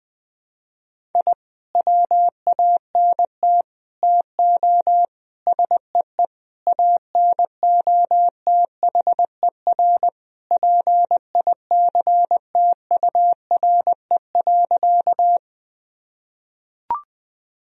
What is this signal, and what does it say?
Morse code, heard as a keyed tone, about 20 wpm, 700 Hz. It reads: I WANT TO SEE ANOTHER PICTURE.
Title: Sentences from Top 1000 Words - Review Part 1 - 20wpm